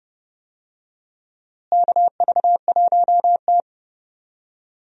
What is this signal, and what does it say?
Morse code, heard as a keyed tone, about 31 wpm, 700 Hz. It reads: KV1T
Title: USA Call Signs - 4 Characters - Repeated 3x - 30wpm